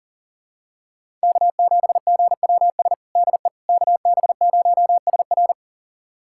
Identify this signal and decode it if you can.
Morse code, heard as a keyed tone, about 40 wpm, 700 Hz. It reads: K7GWS DE KB0SR
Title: USA Callsign DE Callsign - 5 Characters - 40wpm